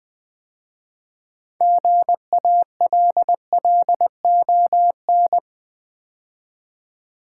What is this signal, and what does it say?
Morse code, heard as a keyed tone, about 20 wpm, 700 Hz. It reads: GALLON
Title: 6 Letter Words - First 3 Letters Given - 20wpm